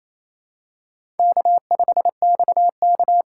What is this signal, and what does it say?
Morse code, heard as a keyed tone, about 28 wpm, 700 Hz. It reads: K5XK
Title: K1USN SST QSO Practice - 28wpm